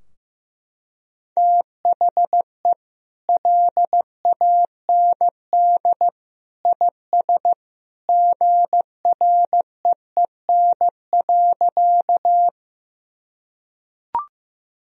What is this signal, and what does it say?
Morse code, heard as a keyed tone, about 15 wpm, 700 Hz. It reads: THE LAND IS GREEN.
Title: Sentences from Top 1000 Words - Review Part 2 - 15wpm